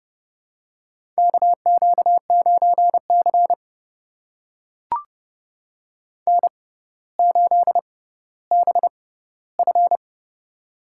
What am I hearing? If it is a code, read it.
Morse code, heard as a keyed tone, about 32 wpm, spaced as for 16 wpm, 700 Hz. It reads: KQ9C N8BF
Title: USA Call Signs - 4 Characters - 6x Character Spacing - 30wpm